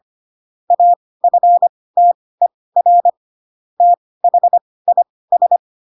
Morse code, heard as a keyed tone, about 25 wpm, spaced as for 18 wpm, 700 Hz. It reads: AFTER THIS
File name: Pinocchio ch3 12 25 ik1hge